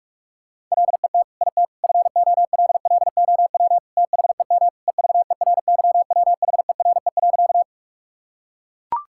Morse code, heard as a keyed tone, about 42 wpm, 700 Hz. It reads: LET IT FOLLOW THEM EVERYWHERE.
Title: Sentences from Top 1600 Words 45wpm